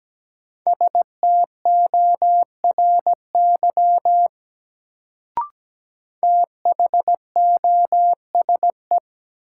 Morse code, heard as a keyed tone, about 17 wpm, 700 Hz. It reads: STORY THOSE
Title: Sentences from Top 300 Words - New Word Drill - 17wpm